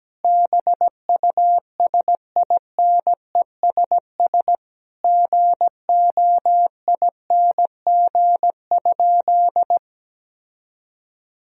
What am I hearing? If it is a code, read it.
Morse code, heard as a keyed tone, about 17 wpm, 700 Hz. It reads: BUSINESS GOING?
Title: Sentences from Top 1000 Words - Review Part 4 - 17wpm